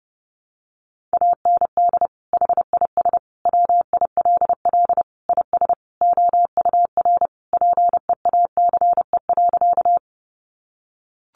Morse code, heard as a keyed tone, about 30 wpm, 700 Hz. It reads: AND HIS WILL IS OUR PEACE.
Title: Sentences from Top 1500 Words 30wpm